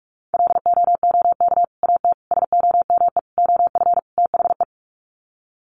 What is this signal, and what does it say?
Morse code, heard as a keyed tone, about 44 wpm, 700 Hz. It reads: LOOK AT SOME OF THE
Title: Sets of 5 Words Encore 45wpm